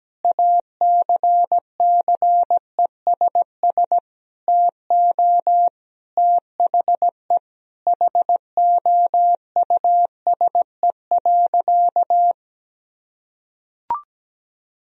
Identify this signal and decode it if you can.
Morse code, heard as a keyed tone, about 17 wpm, 700 Hz. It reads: ACCESS TO THE HOUSE.
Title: Sentences from Top 1000 Words - Review Part 4 - 17wpm